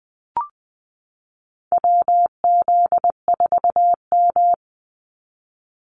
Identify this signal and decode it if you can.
Morse code, heard as a keyed tone, about 20 wpm, 700 Hz. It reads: WZ4M
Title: NA SOTA Activator Callsigns 20wpm